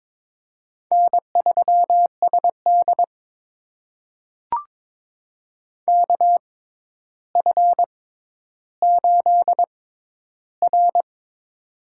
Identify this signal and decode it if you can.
Morse code, heard as a keyed tone, about 23 wpm, spaced as for 11 wpm, 700 Hz. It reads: N3SD KF8R
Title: USA Call Signs - 4 Characters - 6x Character Spacing - 22wpm